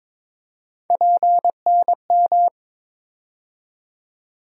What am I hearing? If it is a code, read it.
Morse code, heard as a keyed tone, about 21 wpm, 700 Hz. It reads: PNM
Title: USA Call Sign Postfixes - 3 Letters 22wpm